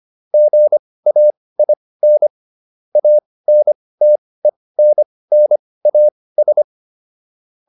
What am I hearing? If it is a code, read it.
Morse code, heard as a keyed tone, about 25 wpm, spaced as for 18 wpm, 600 Hz. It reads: GAIN ANTENNAS